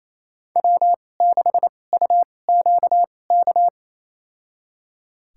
Morse code, heard as a keyed tone, about 28 wpm, spaced as for 20 wpm, 700 Hz. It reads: W6UQK
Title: USA Call Signs - 5 Characters - 2x Character Spacing - 28wpm